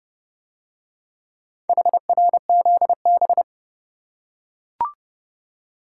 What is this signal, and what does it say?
Morse code, heard as a keyed tone, about 29 wpm, 700 Hz. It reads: HRZB